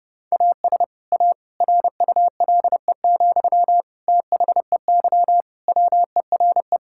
Morse code, heard as a keyed tone, about 32 wpm, 700 Hz. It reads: AS A RULE, THEY WERE
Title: Sentences from Top 1000 Words - Review Part 6 - 30wpm